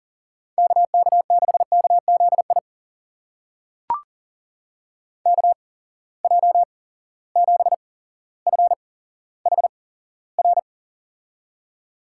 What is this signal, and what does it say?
Morse code, heard as a keyed tone, about 40 wpm, spaced as for 17 wpm, 700 Hz. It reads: KK6KZI KJ7FHR
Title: USA Call Signs - 6 Characters - 8x Character Spacing - 40wpm